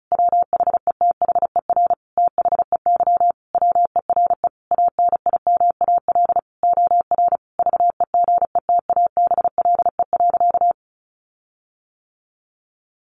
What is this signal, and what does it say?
Morse code, heard as a keyed tone, about 34 wpm, 700 Hz. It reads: WHETHER THEY WERE ANIMAL OR VEGETABLE.
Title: Sentences from Top 1300 Words 35wpm